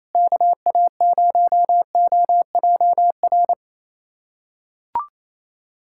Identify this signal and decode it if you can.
Morse code, heard as a keyed tone, about 28 wpm, 700 Hz. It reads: KA0OJR
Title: USA Callsign DE Callsign 28wpm